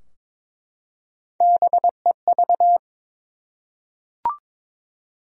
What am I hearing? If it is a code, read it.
Morse code, heard as a keyed tone, about 22 wpm, 700 Hz. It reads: BEV
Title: USA Common Names - 22wpm